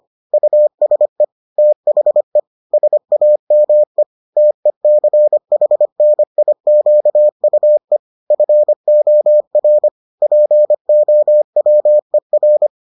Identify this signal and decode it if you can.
Morse code, heard as a keyed tone, about 25 wpm, 600 Hz. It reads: USE THE SAME TECHNIQUE FOR POWER